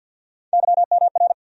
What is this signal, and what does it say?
Morse code, heard as a keyed tone, about 52 wpm, 700 Hz. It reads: YMR
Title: USA Call Sign Postfixes - 3 Letters 50wpm